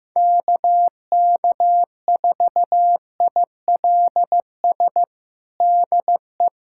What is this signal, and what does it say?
Morse code, heard as a keyed tone, about 15 wpm, 700 Hz. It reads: KK4ILS DE
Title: USA Callsign DE Callsign - 6 Characters - 15wpm